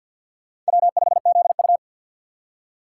Morse code, heard as a keyed tone, about 65 wpm, spaced as for 57 wpm, 700 Hz. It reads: W5ZU
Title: Speed Racing - USA Call Signs 50wpm